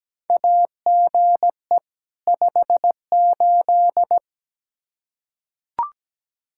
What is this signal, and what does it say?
Morse code, heard as a keyed tone, about 17 wpm, 700 Hz. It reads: AGE 58
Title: Practice QSO - Exchange 3 - More Info - 17wpm